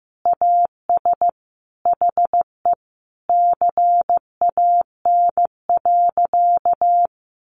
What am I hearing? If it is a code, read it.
Morse code, heard as a keyed tone, about 15 wpm, 700 Hz. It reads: AS HE CAN.
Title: Additional Easy Phrases 15wpm